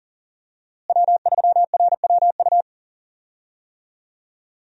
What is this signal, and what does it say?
Morse code, heard as a keyed tone, about 40 wpm, 700 Hz. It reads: W3RWU
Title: USA Call Signs - 5 Characters - Repeated 3x - 40wpm